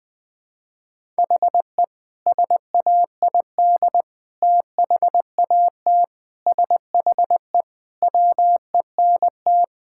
Morse code, heard as a keyed tone, about 20 wpm, 700 Hz. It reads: HE SAID THAT SHE WENT
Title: Sentences with 2-Word Phrases - 20wpm